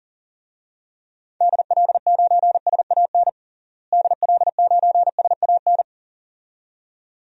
Morse code, heard as a keyed tone, about 40 wpm, 700 Hz. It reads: DL9SAN DL9SAN
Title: DXCC-100 Call Signs - 6 Characters - Repeated 2x - 40wpm